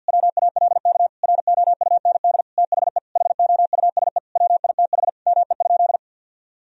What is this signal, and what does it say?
Morse code, heard as a keyed tone, about 51 wpm, 700 Hz. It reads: WALK ROUND THE HOUSE WITH ME?
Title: Sentences from Top 1000 Words - Review Part 1 - 50wpm